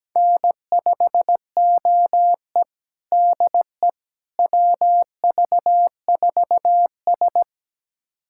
Morse code, heard as a keyed tone, about 17 wpm, 700 Hz. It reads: N5OE DE WV4S
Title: USA Callsign DE Callsign - 4 Characters - 17wpm